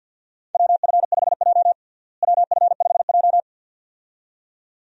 Morse code, heard as a keyed tone, about 49 wpm, 700 Hz. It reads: WF5J WF5J
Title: USA Call Signs - 4 Characters - Repeated 2x - 50wpm